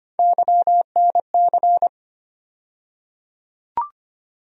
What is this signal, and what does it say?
Morse code, heard as a keyed tone, about 25 wpm, 700 Hz. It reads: YNC
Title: USA Call Sign Postfixes - 3 Letters 25wpm